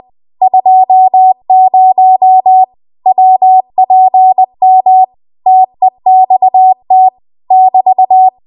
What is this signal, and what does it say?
Morse code, heard as a keyed tone, about 20 wpm, 750 Hz. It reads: 20 WPM TEXT <BT>
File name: Archive / 20 WPM / 20WPM